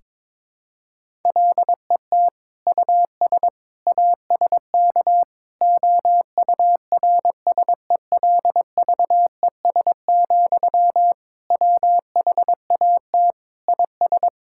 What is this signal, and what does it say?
Morse code, heard as a keyed tone, about 22 wpm, 700 Hz. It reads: LET US ASK OURSELVES, WHAT IS
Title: Sentences from Top 1600 Words 22wpm